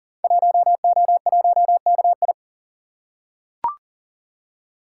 Morse code, heard as a keyed tone, about 39 wpm, 700 Hz. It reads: 1O1KI
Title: Code Groups - 5 Letter:Numbers - 40wpm